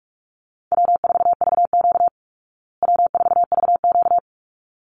Morse code, heard as a keyed tone, about 44 wpm, 700 Hz. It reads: W4VQ W4VQ
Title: Easy Call Signs Repeated 2x - 45wpm